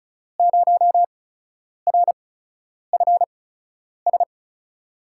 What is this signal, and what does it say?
Morse code, heard as a keyed tone, about 36 wpm, spaced as for 10 wpm, 700 Hz. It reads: 0RFS
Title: USA Call Signs - 5 Characters - 8x Character Spacing - 35wpm